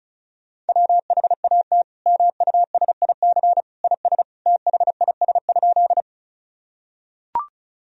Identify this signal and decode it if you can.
Morse code, heard as a keyed tone, about 34 wpm, 700 Hz. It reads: WHAT MUSIC IS THIS?
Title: Sentences from Top 1000 Words - Review Part 1 - 35wpm